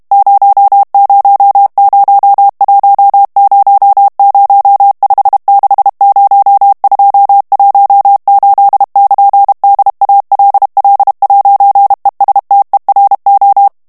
Morse code, heard as a keyed tone, about 32 wpm, 800 Hz. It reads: 000100560218 <KN>DALL'ESTERO
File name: morse mp3 / ITALIEN texte clair 1 vitesse 30 mots minute